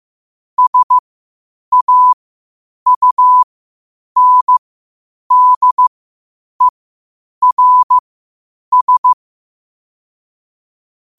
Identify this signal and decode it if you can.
Morse code, heard as a keyed tone, about 15 wpm, spaced as for 8 wpm, 1000 Hz. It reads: SAUNDERS
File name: qotdfiles / 8wpm